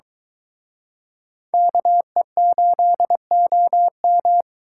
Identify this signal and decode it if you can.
Morse code, heard as a keyed tone, about 22 wpm, 700 Hz. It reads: KE8OM